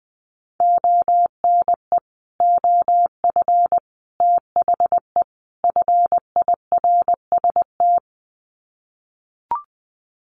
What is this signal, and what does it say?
Morse code, heard as a keyed tone, about 20 wpm, 700 Hz. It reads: ONE OF THE FIRST
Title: Sets of 4 Words 20wpm